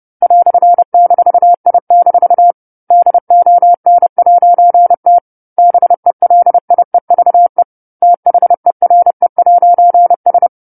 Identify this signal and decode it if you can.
Morse code, heard as a keyed tone, about 30 wpm, 700 Hz. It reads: "-I- DON'T BELIEVE THERE'S
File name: 10-Chapter-10 30 WPM